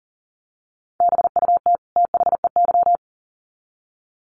Morse code, heard as a keyed tone, about 40 wpm, 700 Hz. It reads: BUT THEY
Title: Sets of 2 Words 40wpm